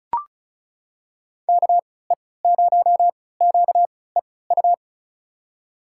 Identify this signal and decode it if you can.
Morse code, heard as a keyed tone, about 36 wpm, spaced as for 20 wpm, 700 Hz. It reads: KE0QEU
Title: USA Call Signs - 6 Characters - 3x Character Spacing - 35wpm